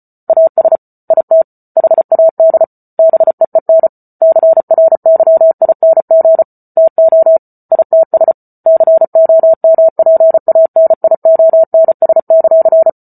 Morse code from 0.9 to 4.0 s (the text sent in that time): IT HAD BEEN